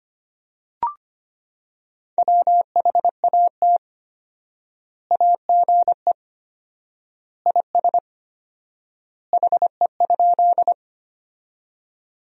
Morse code, heard as a keyed tone, about 25 wpm, 700 Hz. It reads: WHAT AGE IS HE?
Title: 4x Word-Spacing - Sentences from Top 500 Words - 25wpm